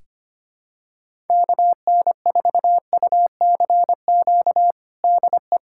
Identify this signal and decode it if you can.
Morse code, heard as a keyed tone, about 25 wpm, 700 Hz. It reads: KN4UCQ DE
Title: USA Callsign DE Callsign - 6 Characters - 25wpm